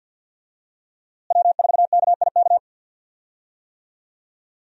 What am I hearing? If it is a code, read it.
Morse code, heard as a keyed tone, about 46 wpm, 700 Hz. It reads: W4KIK
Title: USA Call Signs - 5 Characters - Repeated 2x - 50wpm